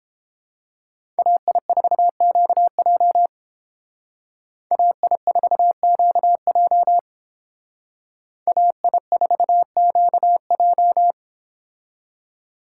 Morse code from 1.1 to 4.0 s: AI4QJ

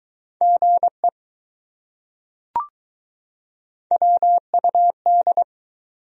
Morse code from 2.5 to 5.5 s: WUD

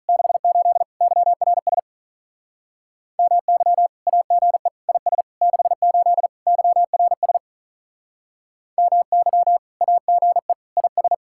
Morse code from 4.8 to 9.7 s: IS 68 YRS MY

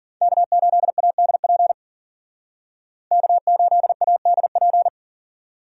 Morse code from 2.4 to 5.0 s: K8ADP

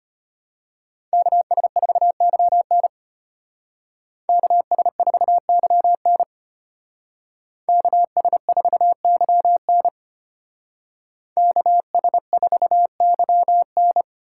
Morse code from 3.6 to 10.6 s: KS4YN KS4YN

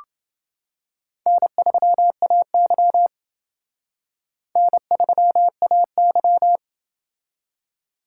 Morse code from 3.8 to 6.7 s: N3AY

